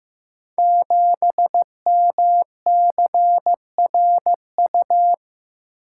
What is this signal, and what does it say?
Morse code, heard as a keyed tone, about 15 wpm, 700 Hz. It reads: 7MCRU
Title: Code Groups - 5 Letter:Numbers - 15wpm